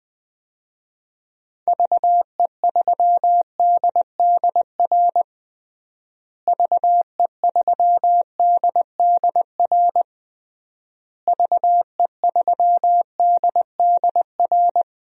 Morse code, heard as a keyed tone, about 20 wpm, 700 Hz. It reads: VE3DDR VE3DDR VE3DDR
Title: DXCC-100 Call Signs - 6 Characters - Repeated 3x - 20wpm